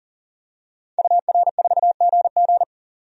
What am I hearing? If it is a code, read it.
Morse code, heard as a keyed tone, about 40 wpm, 700 Hz. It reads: UR4GG
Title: DXCC-100 Call Signs - 5 Characters - 40wpm